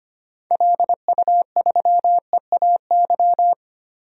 Morse code, heard as a keyed tone, about 25 wpm, 700 Hz. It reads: LU3EAY
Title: DXCC-100 Call Signs - 6 Characters - Repeated 3x - 25wpm